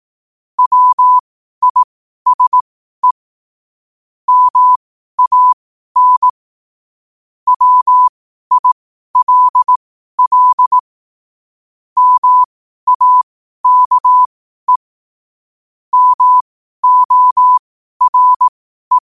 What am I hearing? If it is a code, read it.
Morse code, heard as a keyed tone, about 18 wpm, spaced as for 12 wpm, 1000 Hz. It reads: WISE MAN WILL MAKE MORE
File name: qotdfiles / 12wpm